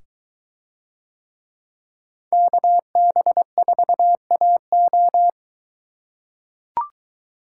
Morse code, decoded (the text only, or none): KB4AO